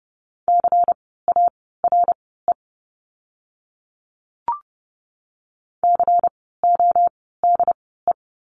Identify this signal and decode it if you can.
Morse code, heard as a keyed tone, about 30 wpm, spaced as for 17 wpm, 700 Hz. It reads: CARE CODE
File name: Lesson 16f - Letter c - Words 30wpm